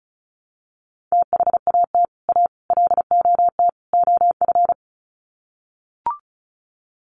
Morse code, heard as a keyed tone, about 34 wpm, 700 Hz. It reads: THAT A LOT OF
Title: Sets of 4 Words 35wpm